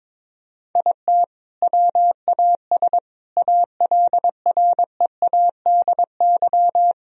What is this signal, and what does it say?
Morse code, heard as a keyed tone, about 22 wpm, 700 Hz. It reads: IT WAS ALREADY
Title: Sentences from Top 1000 Words - Review Part 2 - 22wpm